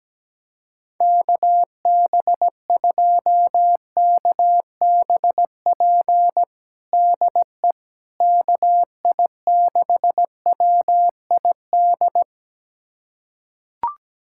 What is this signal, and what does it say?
Morse code, heard as a keyed tone, about 17 wpm, 700 Hz. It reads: KB2KBP DE KI6WID
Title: USA Callsign DE Callsign - 6 Characters - 17wpm